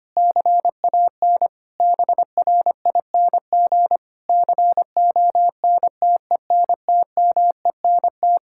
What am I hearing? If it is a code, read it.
Morse code, heard as a keyed tone, about 25 wpm, 700 Hz. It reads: CAN BRING CONTENTMENT
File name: practice / 25wpm / Difficult Sentences with Hints - 25wpm